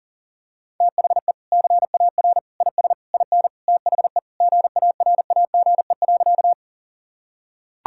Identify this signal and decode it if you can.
Morse code, heard as a keyed tone, about 40 wpm, 700 Hz. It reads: THE CAR IS IN THE GARAGE.
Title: Sentences from Top 1600 Words 40wpm